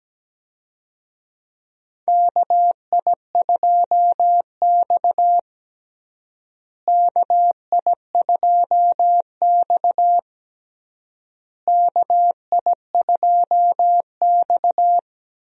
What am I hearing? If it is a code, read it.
Morse code, heard as a keyed tone, about 17 wpm, 700 Hz. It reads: KI2X KI2X KI2X
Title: USA Call Signs - 4 Characters - Repeated 3x - 17wpm